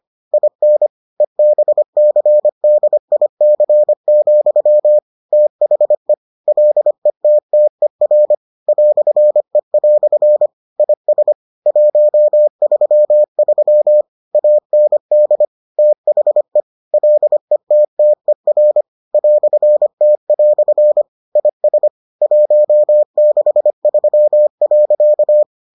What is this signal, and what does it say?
Morse code, heard as a keyed tone, about 25 wpm, 600 Hz. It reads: IN EBCDIC, THE LETTER "E" IS 133 AND THE LETTER "T" IS 163.